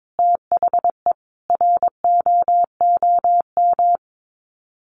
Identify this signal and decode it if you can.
Morse code, heard as a keyed tone, about 22 wpm, 700 Hz. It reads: THE ROOM
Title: Sets of 5 Words Encore 22wpm